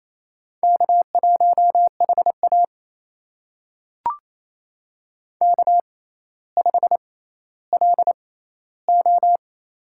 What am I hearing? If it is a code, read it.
Morse code, heard as a keyed tone, about 28 wpm, spaced as for 14 wpm, 700 Hz. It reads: K1HA K5LO
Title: USA Call Signs - 4 Characters - 6x Character Spacing - 28wpm